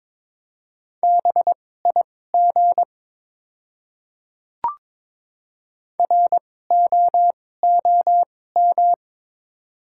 Morse code, heard as a keyed tone, about 22 wpm, spaced as for 16 wpm, 700 Hz. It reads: BIG ROOM
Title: 2x Character Spacing - Top 300 Words - 22wpm